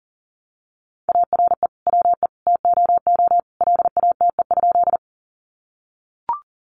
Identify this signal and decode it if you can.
Morse code, heard as a keyed tone, about 40 wpm, 700 Hz. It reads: ARE WE TOO LATE?
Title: Sentences from Top 300 Words 40wpm